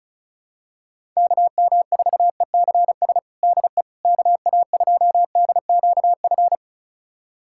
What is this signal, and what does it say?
Morse code, heard as a keyed tone, about 34 wpm, 700 Hz. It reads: KM4ECS DE KA2DQF
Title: USA Callsign DE Callsign - 6 Characters - 35wpm